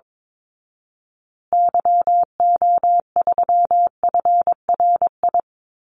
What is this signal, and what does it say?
Morse code, heard as a keyed tone, about 22 wpm, 700 Hz. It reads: YO3FRI
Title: Call Signs Repeated 2x - 22wpm